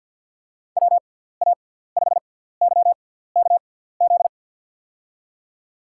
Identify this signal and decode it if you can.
Morse code, heard as a keyed tone, about 52 wpm, spaced as for 17 wpm, 700 Hz. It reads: WA5YKZ
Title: USA Call Signs - 6 Characters - 6x Character Spacing - 50wpm